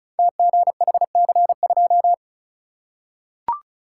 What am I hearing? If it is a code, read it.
Morse code, heard as a keyed tone, about 36 wpm, 700 Hz. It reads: TGHC2
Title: Code Groups - 5 Letter:Numbers - 35wpm